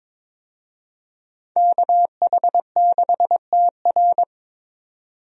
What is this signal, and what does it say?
Morse code, heard as a keyed tone, about 22 wpm, 700 Hz. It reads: KH6TR
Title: DXCC-100 Call Signs - 5 Characters - 22wpm